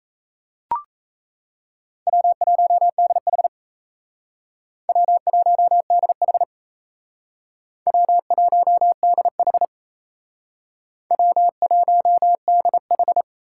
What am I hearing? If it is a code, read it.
Morse code, heard as a keyed tone, about 37 wpm, 700 Hz. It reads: W1DH W1DH W1DH W1DH